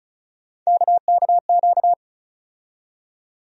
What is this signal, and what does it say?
Morse code, heard as a keyed tone, about 36 wpm, 700 Hz. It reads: KKQ
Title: USA Call Sign Postfixes - 3 Letters 35wpm